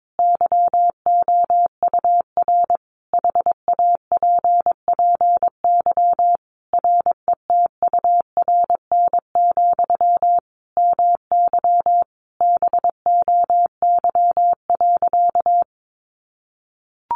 Morse code, heard as a keyed tone, about 22 wpm, 700 Hz. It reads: YOUR HAPPY RETURN, MY BOY.
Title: Sentences from Top 1000 Words 22wpm